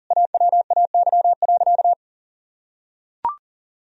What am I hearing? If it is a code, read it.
Morse code, heard as a keyed tone, about 37 wpm, 700 Hz. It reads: AWAY.
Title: Sentences from Top 1000 Words - Review Part 2 - 40wpm